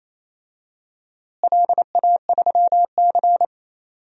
Morse code, heard as a keyed tone, about 28 wpm, 700 Hz. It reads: LA3C
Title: DXCC-100 Call Signs - 4 Characters - Repeated 3x - 28wpm